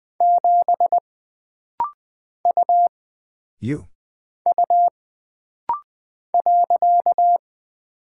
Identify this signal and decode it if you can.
Morse code, heard as a keyed tone, about 20 wpm, 700 Hz. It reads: 7 U U .
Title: Single Character (Letter:Number:Punctuation:Prosign) - Rapid-Fire - 20wpm